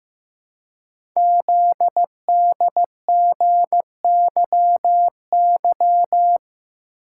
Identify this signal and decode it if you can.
Morse code, heard as a keyed tone, about 15 wpm, 700 Hz. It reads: ZDGYY
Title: Code Groups - 5 Letters - 15wpm